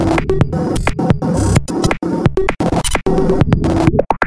There is no code.